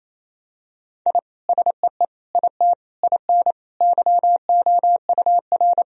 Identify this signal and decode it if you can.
Morse code, heard as a keyed tone, about 28 wpm, 700 Hz. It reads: I SEE IT IN YOUR
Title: Sentences from Top 1000 Words - Review Part 1 - 28wpm